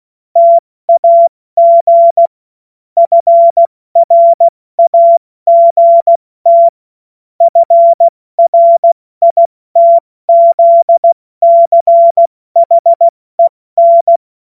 Morse code, heard as a keyed tone, about 16 wpm, 680 Hz. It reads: TAG FRAGT FRITZCHEN
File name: Morsekurs / Witze / Witz 21. September 2021, 70 F80